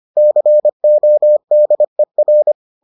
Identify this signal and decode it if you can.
Morse code, heard as a keyed tone, about 25 wpm, 600 Hz. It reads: CODER